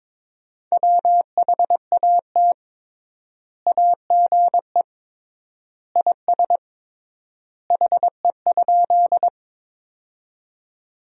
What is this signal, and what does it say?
Morse code, heard as a keyed tone, about 22 wpm, 700 Hz. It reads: WHAT AGE IS HE?
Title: 3x Word-Spacing - Sentences from Top 500 Words - 22wpm